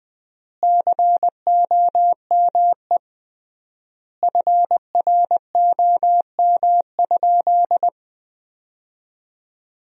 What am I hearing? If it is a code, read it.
Morse code, heard as a keyed tone, about 20 wpm, 700 Hz. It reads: COME FROM?
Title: 3x Word-Spacing - Sentences from Top 200 Words - 20wpm